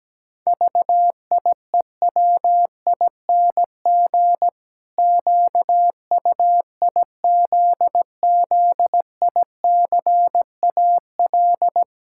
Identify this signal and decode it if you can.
Morse code, heard as a keyed tone, about 17 wpm, 700 Hz. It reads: VIEWING QUIZZICAL
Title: Pangram Challenge v2 - 17wpm